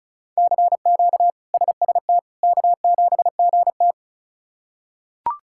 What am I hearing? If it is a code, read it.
Morse code, heard as a keyed tone, about 36 wpm, 700 Hz. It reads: CQ SST K7GT
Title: K1USN SST QSO Practice - 35wpm